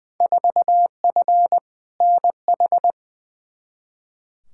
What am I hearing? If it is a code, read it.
Morse code, heard as a keyed tone, about 20 wpm, 700 Hz. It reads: 4F NH
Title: Field Day Exchanges Encore - 20wpm